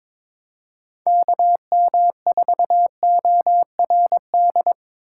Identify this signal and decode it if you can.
Morse code, heard as a keyed tone, about 22 wpm, 700 Hz. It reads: KM4ORD
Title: USA Call Signs - 6 Characters - 2x Character Spacing - 22wpm